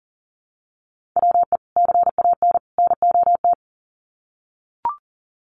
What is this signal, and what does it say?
Morse code, heard as a keyed tone, about 40 wpm, 700 Hz. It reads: WE CAN NOT
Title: Sets of 3 Words 40wpm